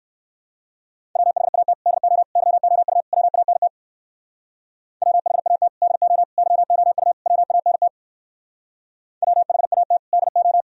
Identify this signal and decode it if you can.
Morse code, heard as a keyed tone, about 60 wpm, 700 Hz. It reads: WHAT DO YOU WANT WHAT DO YOU WANT WHAT DO